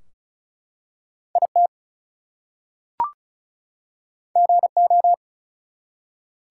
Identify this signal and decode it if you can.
Morse code, heard as a keyed tone, about 34 wpm, 700 Hz. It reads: IT GO
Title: Two Letter Words - 35wpm